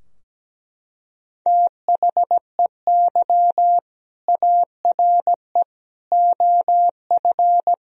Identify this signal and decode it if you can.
Morse code, heard as a keyed tone, about 17 wpm, 700 Hz. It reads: THEY ARE OF
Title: Sentences from Top 1000 Words - Review Part 4 - 17wpm